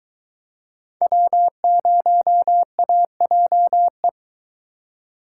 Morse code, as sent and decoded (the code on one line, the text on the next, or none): .-- ----- .- .--- .
W0AJE